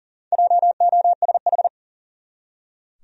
Morse code, heard as a keyed tone, about 40 wpm, 700 Hz. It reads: JOSH
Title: USA Common Names - 40wpm